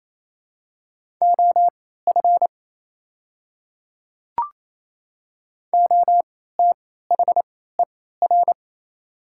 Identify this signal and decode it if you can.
Morse code, heard as a keyed tone, about 28 wpm, spaced as for 16 wpm, 700 Hz. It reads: OF OTHER
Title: 3x Character Spacing - Top 100 Words - 28wpm